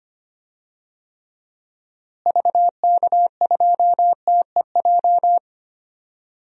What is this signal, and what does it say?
Morse code, heard as a keyed tone, about 25 wpm, 700 Hz. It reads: VK2TEJ
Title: DXCC-100 Call Signs - 6 Characters - Repeated 2x - 25wpm